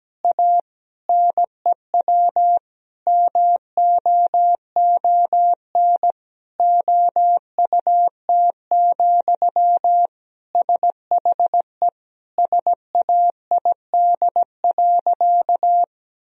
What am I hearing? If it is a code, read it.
Morse code, heard as a keyed tone, about 17 wpm, 700 Hz. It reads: A NEW MOON OUT, SHE SAID.
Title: Sentences from Top 1000 Words - Review Part 3 - 17wpm